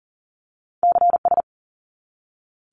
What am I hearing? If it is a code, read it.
Morse code, heard as a keyed tone, about 41 wpm, 700 Hz. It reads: CS
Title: Abbreviations 40wpm